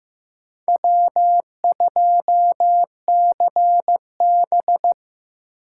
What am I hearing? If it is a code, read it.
Morse code, heard as a keyed tone, about 15 wpm, 700 Hz. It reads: W2CB
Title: USA Call Signs - 4 Characters - 6x Character Spacing - 15wpm